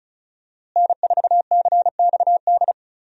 Morse code, heard as a keyed tone, about 36 wpm, 700 Hz. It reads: N4CXD
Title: USA Call Signs - 5 Characters - Repeated 2x - 35wpm